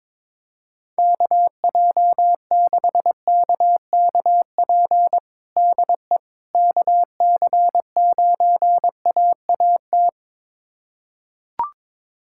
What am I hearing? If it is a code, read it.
Morse code, heard as a keyed tone, about 22 wpm, 700 Hz. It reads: KJ6KKP DE KC9AAT
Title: USA Callsign DE Callsign - 6 Characters - 22wpm